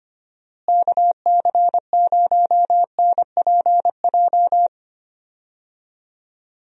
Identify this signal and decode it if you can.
Morse code, heard as a keyed tone, about 25 wpm, 700 Hz. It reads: KC0NPJ
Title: USA Call Signs - 6 Characters - Repeated 3x - 25wpm